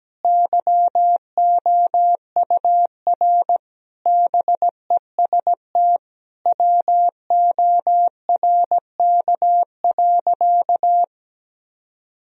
Morse code, heard as a keyed tone, about 17 wpm, 700 Hz. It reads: YOUR BEST WORK.
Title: Sentences from Top 1000 Words - Review Part 3 - 17wpm